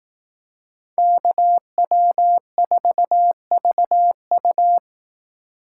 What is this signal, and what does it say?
Morse code, heard as a keyed tone, about 18 wpm, 700 Hz. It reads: KW4VU
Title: Speed Racing - USA 2x2 Call Signs 15wpm